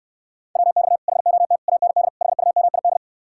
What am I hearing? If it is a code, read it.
Morse code, heard as a keyed tone, about 68 wpm, 700 Hz. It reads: WX HOT AND HUMID